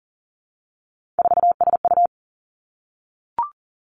Morse code, heard as a keyed tone, about 37 wpm, 700 Hz. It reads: 4SU
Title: Three Challenging Letter-Number Combinations 40wpm